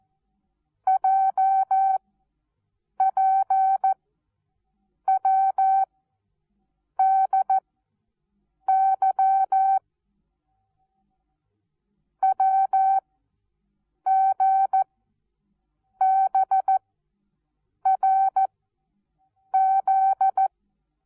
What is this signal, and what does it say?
Morse code, heard as a keyed tone, about 14 wpm, 770 Hz. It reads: J P W D Y W G B R Z
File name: Onlinelehrgang / morsen / DJ4UF Morsen19